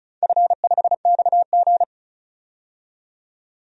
Morse code, heard as a keyed tone, about 36 wpm, 700 Hz. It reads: F5XG
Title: DXCC-100 Call Signs - 4 Characters - 35wpm